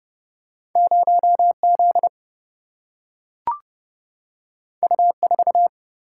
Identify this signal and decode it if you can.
Morse code, heard as a keyed tone, about 30 wpm, 700 Hz. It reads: 0Z U4
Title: Single-Letter Single-Number 30wpm